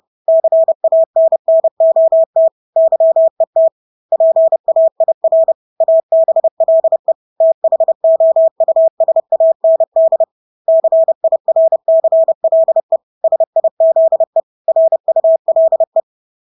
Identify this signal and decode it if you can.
Morse code, heard as a keyed tone, about 30 wpm, 650 Hz. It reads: CANNOT YET PAIR ABLE THOUSAND CIRCLE SIZE RULE
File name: Engelske ord 30-30 wpm